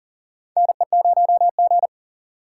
Morse code, heard as a keyed tone, about 40 wpm, 690 Hz. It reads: NE0G
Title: USA Call Signs - 4 Characters - 2x Character Spacing - 40wpm